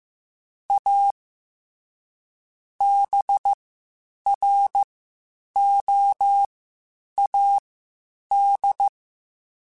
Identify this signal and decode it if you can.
Morse code, heard as a keyed tone, about 15 wpm, spaced as for 8 wpm, 800 Hz. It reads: A BROAD
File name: Spaced / Chuckles Spaced-mp3s / MiddleAge sp @13